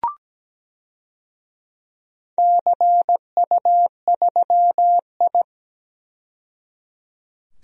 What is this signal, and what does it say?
Morse code, heard as a keyed tone, about 17 wpm, 700 Hz. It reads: CU3I
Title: DXCC-100 Call Signs - 4 Characters - 17wpm